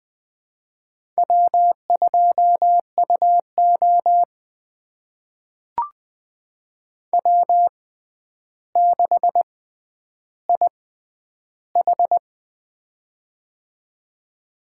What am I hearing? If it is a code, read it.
Morse code, heard as a keyed tone, about 21 wpm, spaced as for 10 wpm, 700 Hz. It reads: W2UO W6IH
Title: USA Call Signs - 4 Characters - 6x Character Spacing - 20wpm